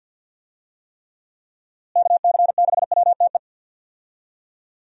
Morse code, heard as a keyed tone, about 45 wpm, 700 Hz. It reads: KC6WTE
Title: USA Call Signs - 6 Characters - Repeated 2x - 50wpm